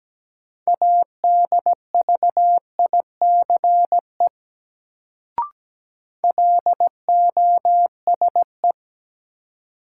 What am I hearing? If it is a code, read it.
Morse code, heard as a keyed tone, about 17 wpm, 700 Hz. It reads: ADVICE LOSE
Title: Sentences from Top 1100 Words - New Word Drill - 17wpm